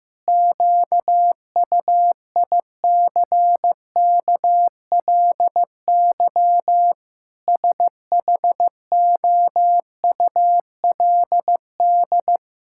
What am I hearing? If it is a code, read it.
Morse code, heard as a keyed tone, about 15 wpm, 700 Hz. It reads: QUICKLY SHOULD
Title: Sentences from Top 1000 Words - Review Part 5 - 15wpm